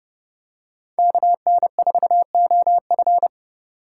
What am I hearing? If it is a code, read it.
Morse code, heard as a keyed tone, about 30 wpm, 700 Hz. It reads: KN4OF
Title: USA Call Signs - 5 Characters - 2x Character Spacing - 30wpm